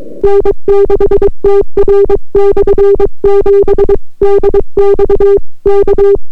Morse code, heard as a keyed tone, about 23 wpm, 400 Hz. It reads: N6TR/7 DX K